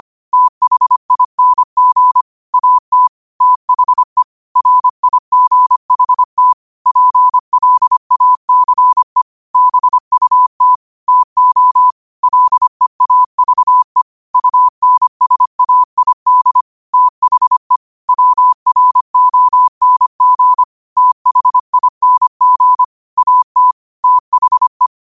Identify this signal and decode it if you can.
Morse code, heard as a keyed tone, about 25 wpm, 1000 Hz. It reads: THING AT THE RIGHT PLACE BUT TO LEAVE UNSAID THE WRONG THING AT THE
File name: qotdfiles / 25wpm